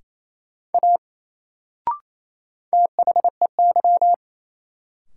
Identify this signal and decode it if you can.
Morse code, heard as a keyed tone, about 29 wpm, 700 Hz. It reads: A THEY
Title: Sentences from Top 100 Words - New Word Drill - 28wpm